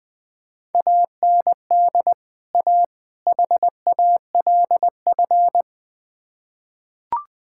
Morse code, heard as a keyed tone, about 20 wpm, 700 Hz. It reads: AND A HALF
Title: Build A Phrase - 3-Word Phrases - 20wpm